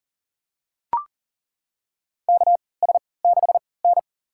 Morse code, heard as a keyed tone, about 40 wpm, spaced as for 23 wpm, 700 Hz. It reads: KS6N
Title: USA Call Signs - 4 Characters - 3x Character Spacing - 40wpm